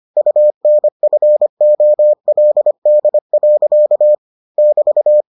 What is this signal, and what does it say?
Morse code, heard as a keyed tone, about 25 wpm, 600 Hz. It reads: UNFOLD. <BT>